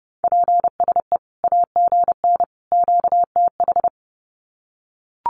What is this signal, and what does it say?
Morse code, heard as a keyed tone, about 30 wpm, 700 Hz. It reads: PSE AGN QTH
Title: Common QSO Phrases 30wpm